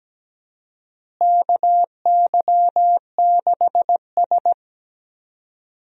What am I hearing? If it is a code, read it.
Morse code, heard as a keyed tone, about 17 wpm, 700 Hz. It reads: KY6S
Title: USA Call Signs - 4 Characters - Repeated 3x - 17wpm